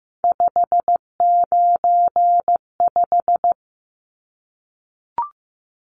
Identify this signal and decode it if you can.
Morse code, heard as a keyed tone, about 15 wpm, 700 Hz. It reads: 595
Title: Numbers - 3-Digit 15wpm